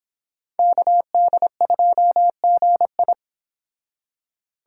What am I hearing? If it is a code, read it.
Morse code, heard as a keyed tone, about 30 wpm, 700 Hz. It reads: KD2GI